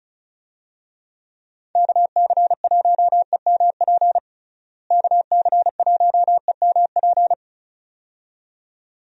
Morse code, heard as a keyed tone, about 36 wpm, 700 Hz. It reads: KC1EMP KC1EMP
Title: USA Call Signs - 6 Characters - Repeated 2x - 35wpm